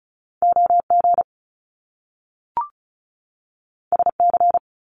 Morse code, heard as a keyed tone, about 36 wpm, 700 Hz. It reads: OG SC
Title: Top Two Letter Combinations in Words 35wpm